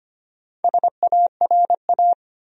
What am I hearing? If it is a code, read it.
Morse code, heard as a keyed tone, about 25 wpm, 700 Hz. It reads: SARA
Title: USA Common Names - Max 4 Letters - 25wpm